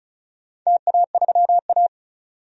Morse code, heard as a keyed tone, about 36 wpm, 700 Hz. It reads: TA3A
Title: DXCC-100 Call Signs - 4 Characters - 35wpm